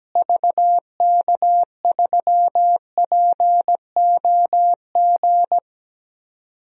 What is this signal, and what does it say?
Morse code, heard as a keyed tone, about 17 wpm, 700 Hz. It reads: VK3POG
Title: DXCC-100 Call Signs - 6 Characters - Repeated 2x - 17wpm